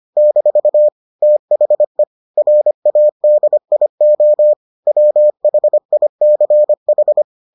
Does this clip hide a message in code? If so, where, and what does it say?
Yes: Morse code from 0.2 to 7.2 s, heard as a keyed tone, about 25 wpm, 600 Hz. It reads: - THE RADIO WHICH